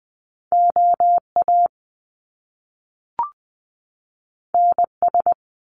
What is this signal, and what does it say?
Morse code, heard as a keyed tone, about 20 wpm, 700 Hz. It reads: OA NS
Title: Top Two Letter Combinations in Words 20wpm